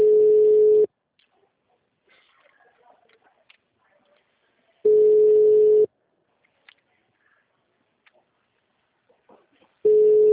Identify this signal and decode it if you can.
Ringback tone (Europe, 425 Hz).
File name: out- -2004- - - .72